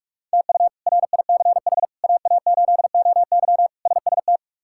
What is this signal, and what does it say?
Morse code, heard as a keyed tone, about 46 wpm, 700 Hz. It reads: TU RICH AA8OY SST
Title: K1USN SST QSO Practice - 45wpm